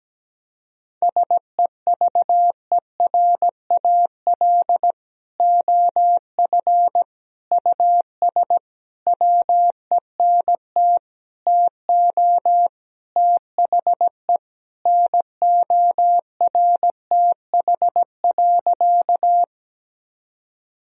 Morse code, heard as a keyed tone, about 17 wpm, 700 Hz. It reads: SEVERAL OF US WENT TO THE NORTH.
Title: Sentences from Top 1000 Words - Review Part 2 - 17wpm